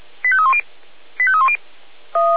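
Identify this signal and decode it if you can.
Five-tone selective call (ZVEI), heard as quick runs of five tones, tones of about 70 ms. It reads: 85310 85310
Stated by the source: ZVEI five-tone selective call received on an SDR, fire brigade.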